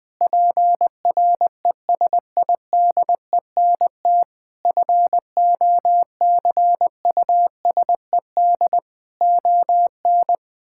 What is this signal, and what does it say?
Morse code, heard as a keyed tone, about 20 wpm, 700 Hz. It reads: PRESIDENT FOCUSED ON
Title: Sentences with 7-Word Phrases - 20wpm